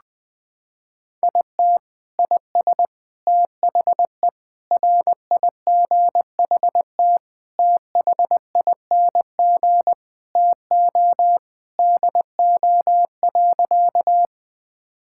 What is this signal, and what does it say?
Morse code, heard as a keyed tone, about 20 wpm, 700 Hz. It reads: IT IS THE RIGHT THING TO DO.
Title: Sentences with 5-Word Phrases - 20wpm